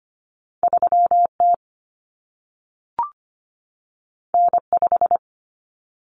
Morse code, heard as a keyed tone, about 25 wpm, 700 Hz. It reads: 3T N5
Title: Common Cut Numbers - 2-Digit 25wpm